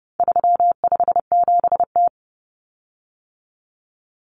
Morse code, heard as a keyed tone, about 30 wpm, 700 Hz. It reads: 357T
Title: Common Cut Numbers - 4-Digit 30wpm